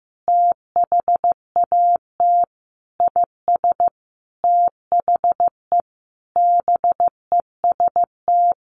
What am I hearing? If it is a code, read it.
Morse code, heard as a keyed tone, about 15 wpm, 700 Hz. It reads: THAT IS THE BEST
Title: Sentences from Top 1200 Words 15wpm